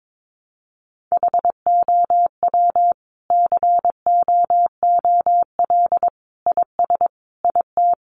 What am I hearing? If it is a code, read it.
Morse code, heard as a keyed tone, about 22 wpm, 700 Hz. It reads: HOW COOL IS IT
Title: Sentences from Top 1200 Words 22wpm